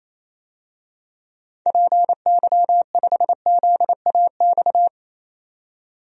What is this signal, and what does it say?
Morse code, heard as a keyed tone, about 28 wpm, 700 Hz. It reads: PY5ZAX
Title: DXCC-100 Call Signs - 6 Characters - 28wpm